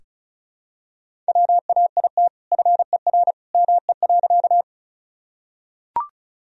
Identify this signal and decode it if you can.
Morse code, heard as a keyed tone, about 34 wpm, 700 Hz. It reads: WAIT FER ME.
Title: Sentences from Top 500 Words and Abbreviations 35wpm